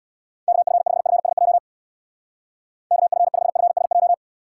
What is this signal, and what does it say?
Morse code, heard as a keyed tone, about 66 wpm, 700 Hz. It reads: KF5FSJ KF5FSJ